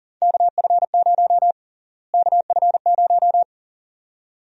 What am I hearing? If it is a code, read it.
Morse code, heard as a keyed tone, about 40 wpm, 700 Hz. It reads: KF0 KF0
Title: USA Call Sign Prefixes - Repeated 2x - 40wpm